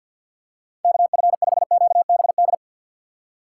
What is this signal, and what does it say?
Morse code, heard as a keyed tone, about 51 wpm, 700 Hz. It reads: KF5QBD
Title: USA Call Signs - 6 Characters - 3x Character Spacing - 50wpm